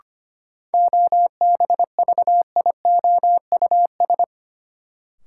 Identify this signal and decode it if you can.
Morse code, heard as a keyed tone, about 25 wpm, 700 Hz. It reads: OBVIOUS
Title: Sentences from Top 1600 Words - New Word Drill - 25wpm